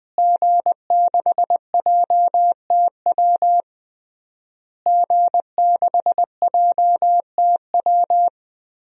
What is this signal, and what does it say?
Morse code, heard as a keyed tone, about 20 wpm, 700 Hz. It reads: G6JTW G6JTW
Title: DXCC-100 Call Signs - 5 Characters - Repeated 3x - 20wpm